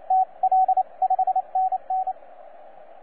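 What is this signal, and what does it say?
Morse code, heard as a keyed tone, about 28 wpm, 750 Hz, with radio noise. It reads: T L 5NN
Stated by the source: TZ6BB on 12 metres CW, from DX World.net.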